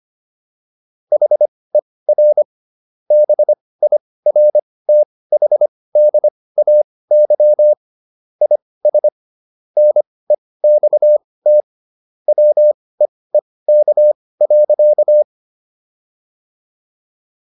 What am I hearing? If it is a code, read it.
Morse code, heard as a keyed tone, about 25 wpm, spaced as for 18 wpm, 600 Hz. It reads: HER BIRTHDAY IS NEXT WEEK.